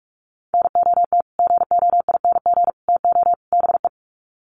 Morse code, heard as a keyed tone, about 44 wpm, 700 Hz. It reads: NOT GOING TO BE
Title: Sets of 4 Words 45wpm